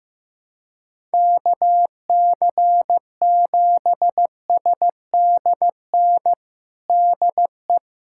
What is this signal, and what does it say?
Morse code, heard as a keyed tone, about 15 wpm, 700 Hz. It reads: KC7SDN DE
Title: USA Callsign DE Callsign - 6 Characters - 15wpm